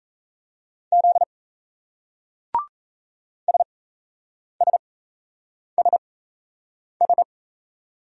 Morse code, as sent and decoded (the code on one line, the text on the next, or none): --.. / ... ... ... ...
Z SSSS